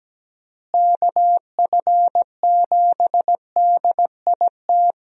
Morse code, heard as a keyed tone, about 17 wpm, 700 Hz. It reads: KF7DIT
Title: USA Call Signs - 6 Characters - Repeated 3x - 17wpm